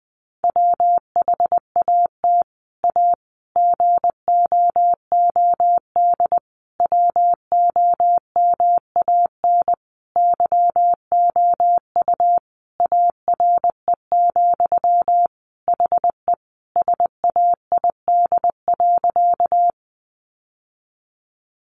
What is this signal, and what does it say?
Morse code, heard as a keyed tone, about 20 wpm, 700 Hz. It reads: WHAT A GOOD WOMAN YOU ARE, HE SAID.
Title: Sentences from Top 1100 Words 20wpm